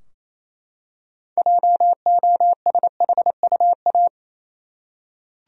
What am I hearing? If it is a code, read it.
Morse code, heard as a keyed tone, about 28 wpm, 700 Hz. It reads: JOSHUA
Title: USA Common Names - 28wpm